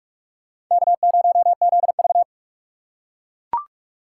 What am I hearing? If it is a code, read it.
Morse code, heard as a keyed tone, about 43 wpm, 700 Hz. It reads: K0ZV
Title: USA Call Signs - 4 Characters - 8x Character Spacing - 45wpm